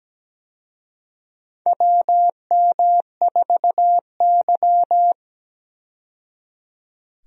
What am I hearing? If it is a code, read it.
Morse code, heard as a keyed tone, about 17 wpm, 700 Hz. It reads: WM4Y
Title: USA Call Signs - 4 Characters - 17wpm